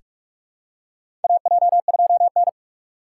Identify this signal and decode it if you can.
Morse code, heard as a keyed tone, about 42 wpm, 700 Hz. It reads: AJ2N
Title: USA Call Signs - 4 Characters - 6x Character Spacing - 45wpm